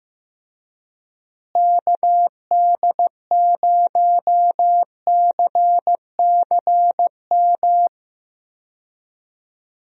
Morse code, heard as a keyed tone, about 15 wpm, 700 Hz. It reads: KD0CCM
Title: USA Call Signs - 6 Characters - Repeated 2x - 15wpm